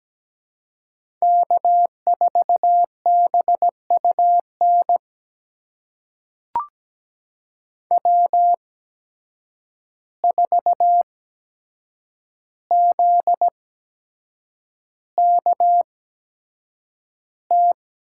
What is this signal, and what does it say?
Morse code, heard as a keyed tone, about 17 wpm, spaced as for 7 wpm, 700 Hz. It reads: K4BUN W4ZKT
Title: USA Call Signs - 5 Characters - 8x Character Spacing - 17wpm